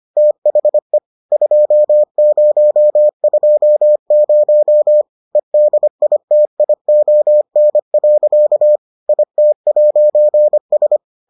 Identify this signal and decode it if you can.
Morse code, heard as a keyed tone, about 25 wpm, 600 Hz. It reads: THE 2020 EDITION. IT'S